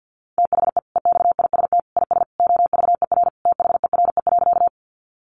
Morse code, heard as a keyed tone, about 48 wpm, 700 Hz. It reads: THE EXIST IS OVER THERE.
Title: Sentences from Top 1100 Words 50wpm